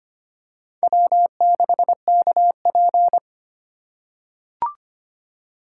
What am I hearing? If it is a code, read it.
Morse code, heard as a keyed tone, about 25 wpm, 700 Hz. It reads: W6KP
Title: USA Call Signs - 4 Characters - 4x Character Spacing - 25wpm